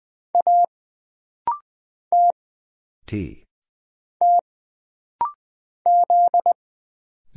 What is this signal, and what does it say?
Morse code, heard as a keyed tone, about 20 wpm, 700 Hz. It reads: A T T Z
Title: Single Letter-Number Alt - Rapid-Fire 20wpm